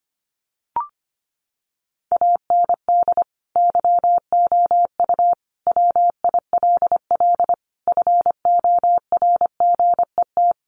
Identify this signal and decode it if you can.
Morse code, heard as a keyed tone, about 25 wpm, 700 Hz. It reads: AND YOU WILL FORGET
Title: Sentences from Top 1200 Words 25wpm